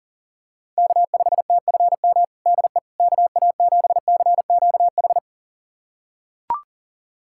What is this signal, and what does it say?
Morse code, heard as a keyed tone, about 40 wpm, 700 Hz. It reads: K5TFM DE KA7CQH
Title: USA Callsign DE Callsign 40wpm